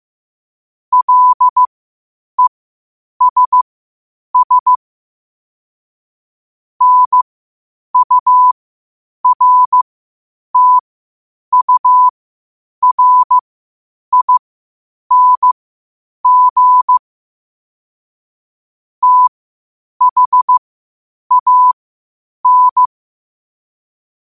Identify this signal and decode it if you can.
Morse code, heard as a keyed tone, about 15 wpm, spaced as for 8 wpm, 1000 Hz. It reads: LESS NURTURING THAN
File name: qotdfiles / 8wpm